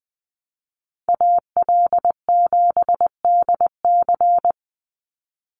Morse code, heard as a keyed tone, about 20 wpm, 700 Hz. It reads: AL7DC
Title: USA Call Signs 20wpm